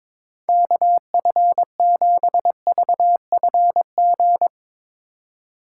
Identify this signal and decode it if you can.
Morse code, heard as a keyed tone, about 22 wpm, 700 Hz. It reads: KF7VFG
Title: Speed Racing - USA 2x3 Call Signs 22wpm